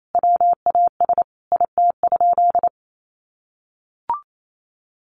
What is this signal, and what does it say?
Morse code, heard as a keyed tone, about 28 wpm, 700 Hz. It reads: WAS IT?
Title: Sentences from Top 300 Words 28wpm